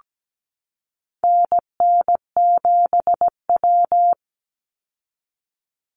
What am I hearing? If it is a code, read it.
Morse code, heard as a keyed tone, about 17 wpm, 700 Hz. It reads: NN7W
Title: Call Signs - 17wpm